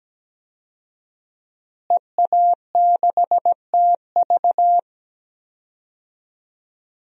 Morse code, heard as a keyed tone, about 17 wpm, 700 Hz. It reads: EA6TV
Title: DXCC-100 Call Signs - 5 Characters - Repeated 2x - 17wpm